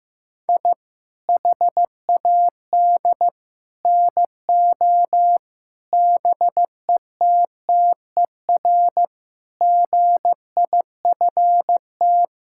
Morse code, heard as a keyed tone, about 15 wpm, 700 Hz. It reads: I HAD NO BETTER GIFT
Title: Sentences from Top 1000 Words - Review Part 6 - 15wpm